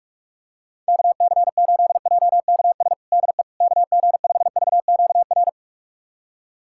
Morse code, heard as a keyed tone, about 46 wpm, 700 Hz. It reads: KC8JKS DE KG5VQR
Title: USA Callsign DE Callsign - 6 Characters - 45wpm